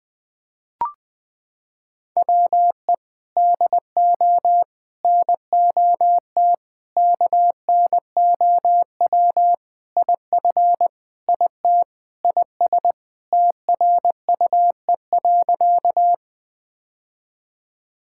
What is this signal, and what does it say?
Morse code, heard as a keyed tone, about 20 wpm, 700 Hz. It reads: WE DO NOT KNOW IF IT IS TRUE.
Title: Sentences with 5-Word Phrases - 20wpm